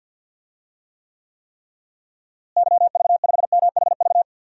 Morse code, heard as a keyed tone, about 50 wpm, 700 Hz. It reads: YV5MHV
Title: DXCC-100 Call Signs - 6 Characters - Repeated 3x - 50wpm